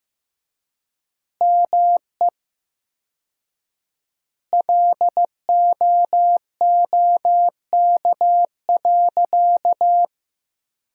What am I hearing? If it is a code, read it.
Morse code, heard as a keyed tone, about 15 wpm, 700 Hz. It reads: ME LOOK.
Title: 4x Word-Spacing - Sentences from Top 300 Words - 15wpm